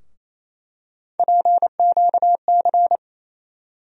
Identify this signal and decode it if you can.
Morse code, heard as a keyed tone, about 27 wpm, 700 Hz. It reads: PQC
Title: USA Call Sign Postfixes - 3 Letters 28wpm